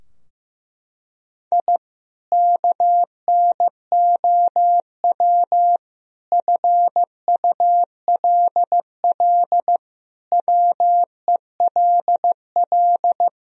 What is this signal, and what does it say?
Morse code, heard as a keyed tone, about 15 wpm, 700 Hz. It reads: I KNOW FULL WELL
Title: Sentences from Top 1000 Words - Review Part 3 - 15wpm